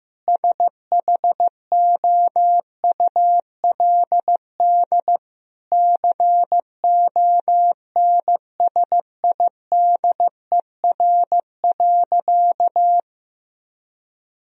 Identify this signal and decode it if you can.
Morse code, heard as a keyed tone, about 15 wpm, 700 Hz. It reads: SHOULD CONSIDER.
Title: Sentences from Top 1000 Words - Review Part 4 - 15wpm